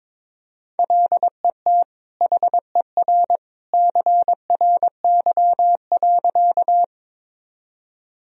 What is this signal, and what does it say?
Morse code, heard as a keyed tone, about 22 wpm, 700 Hz. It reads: LET HER CRY.
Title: Sentences from Top 1000 Words - Review Part 2 - 22wpm